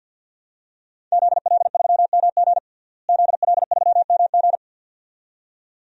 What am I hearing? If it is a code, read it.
Morse code, heard as a keyed tone, about 50 wpm, 700 Hz. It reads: ZL3MG ZL3MG
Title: DXCC-100 Call Signs - 5 Characters - Repeated 2x - 50wpm